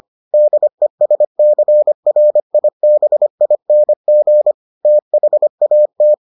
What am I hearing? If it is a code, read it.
Morse code, heard as a keyed tone, about 25 wpm, 600 Hz. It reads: DESCRIBING THAT